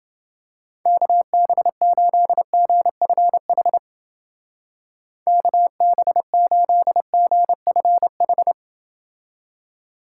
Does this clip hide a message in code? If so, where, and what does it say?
Yes: Morse code from 0.9 to 8.5 s, heard as a keyed tone, about 29 wpm, 700 Hz. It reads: KB8GFH KB8GFH